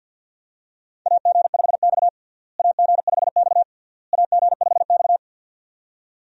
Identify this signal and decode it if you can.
Morse code, heard as a keyed tone, about 52 wpm, 700 Hz. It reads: AG5X AG5X AG5X
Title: USA Call Signs - 4 Characters - Repeated 3x - 50wpm